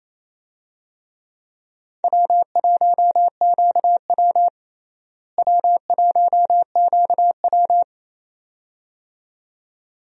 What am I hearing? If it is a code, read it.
Morse code, heard as a keyed tone, about 28 wpm, 700 Hz. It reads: W1QW W1QW
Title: USA Call Signs - 4 Characters - Repeated 2x - 28wpm